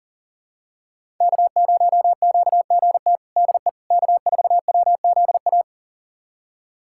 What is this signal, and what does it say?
Morse code, heard as a keyed tone, about 40 wpm, 700 Hz. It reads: K0QGT DE K4WZA
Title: USA Callsign DE Callsign - 5 Characters - 40wpm